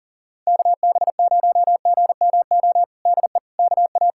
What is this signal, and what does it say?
Morse code, heard as a keyed tone, about 40 wpm, 700 Hz. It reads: KB0GMO DE KA
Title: USA Callsign DE Callsign 40wpm